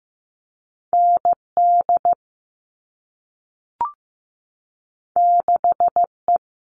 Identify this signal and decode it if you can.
Morse code, heard as a keyed tone, about 15 wpm, 700 Hz. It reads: ND 6E
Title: All Cut Numbers - 2-Digit 15wpm